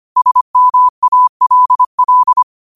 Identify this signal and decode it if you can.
Morse code, heard as a keyed tone, about 25 wpm, 1000 Hz. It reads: SMALL
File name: qotdfiles / 25wpm